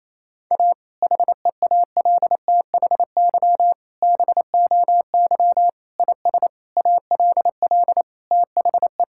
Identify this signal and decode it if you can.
Morse code, heard as a keyed tone, about 27 wpm, 700 Hz. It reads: A HEALTHY BOY IS ALL THE
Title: Sentences from Top 1000 Words - Review Part 5 - 28wpm